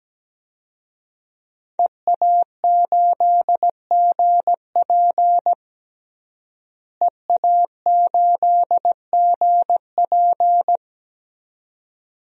Morse code, heard as a keyed tone, about 17 wpm, 700 Hz. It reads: EA8GP EA8GP
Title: DXCC-100 Call Signs - 5 Characters - Repeated 2x - 17wpm